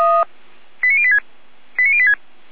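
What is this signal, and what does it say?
Five-tone selective call (ZVEI), heard as quick runs of five tones, tones of about 70 ms. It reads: 89086 89086
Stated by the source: ZVEI five-tone call received by SDR.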